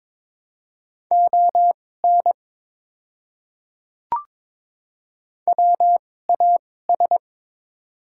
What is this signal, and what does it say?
Morse code, heard as a keyed tone, about 22 wpm, spaced as for 16 wpm, 690 Hz. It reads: ON WAS
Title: 2x Character Spacing - Top 100 Words - 22wpm